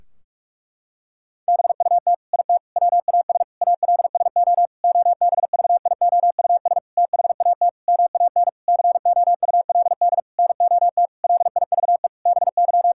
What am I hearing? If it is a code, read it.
Morse code, heard as a keyed tone, about 46 wpm, 700 Hz. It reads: BUT IT WAS ALSO OBVIOUS THAT MAN COULD NOT LIVE BY
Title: Sentences from Top 1600 Words 45wpm